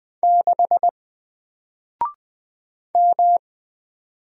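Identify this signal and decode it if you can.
Morse code, heard as a keyed tone, about 20 wpm, 700 Hz. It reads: 6 M
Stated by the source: Single letter and number Morse drill at 20 wpm, from N4LCD.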